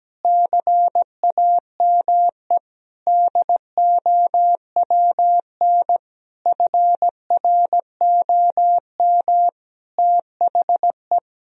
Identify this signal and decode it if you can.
Morse code, heard as a keyed tone, about 17 wpm, 700 Hz. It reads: CAME DOWN FROM THE
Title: Sentences from Top 1000 Words - Review Part 1 - 17wpm